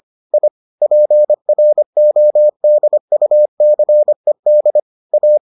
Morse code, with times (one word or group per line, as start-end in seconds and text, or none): 0.3-0.5 I
0.8-4.8 PRODUCED
5.1-5.4 A